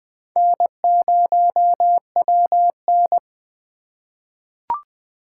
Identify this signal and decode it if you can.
Morse code, heard as a keyed tone, about 21 wpm, 700 Hz. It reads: N0WN
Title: USA Call Signs - 4 Characters - 2x Character Spacing - 20wpm